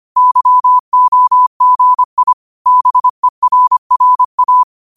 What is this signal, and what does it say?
Morse code, heard as a keyed tone, about 25 wpm, 1000 Hz. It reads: YOGI BERRA
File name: qotdfiles / 25wpm